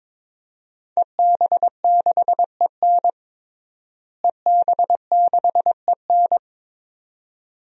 Morse code, heard as a keyed tone, about 22 wpm, 700 Hz. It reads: EB6EN EB6EN
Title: DXCC-100 Call Signs - 5 Characters - Repeated 2x - 22wpm